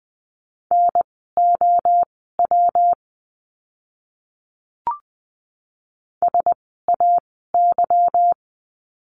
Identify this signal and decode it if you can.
Morse code, heard as a keyed tone, about 20 wpm, spaced as for 14 wpm, 700 Hz. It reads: NOW SAY
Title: 230 Lesson 37f - Letter x - Words 20wpm